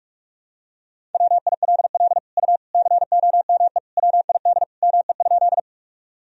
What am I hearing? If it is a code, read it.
Morse code, heard as a keyed tone, about 46 wpm, 700 Hz. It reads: WILL U COME WID ME?
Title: Sentences from Top 200 Words and Abbreviations 45wpm